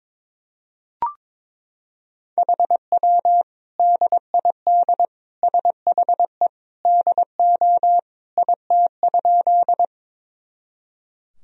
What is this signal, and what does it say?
Morse code, heard as a keyed tone, about 22 wpm, 700 Hz. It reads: HW DID SHE DO IT?
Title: Sentences from Top 100 Words and Abbreviations 22wpm